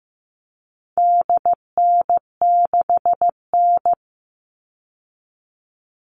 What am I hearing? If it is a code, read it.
Morse code, heard as a keyed tone, about 15 wpm, 700 Hz. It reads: DN6N
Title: All Cut Numbers - 4-Digit 15wpm